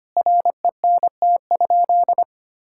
Morse code, heard as a keyed tone, about 25 wpm, 700 Hz. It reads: RENT?
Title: Sentences from Top 1000 Words - Review Part 5 - 25wpm